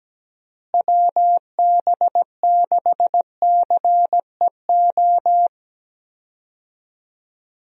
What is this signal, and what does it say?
Morse code, heard as a keyed tone, about 17 wpm, 700 Hz. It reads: WB6CEO
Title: USA Call Signs - 6 Characters - Repeated 3x - 17wpm